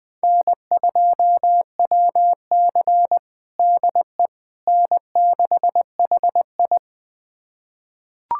Morse code, heard as a keyed tone, about 20 wpm, 700 Hz. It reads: N2WC DE N6HI
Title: USA Callsign DE Callsign - 4 Characters - 20wpm